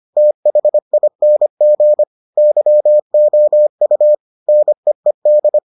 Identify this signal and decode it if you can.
Morse code, heard as a keyed tone, about 25 wpm, 600 Hz. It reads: THING YOU NEED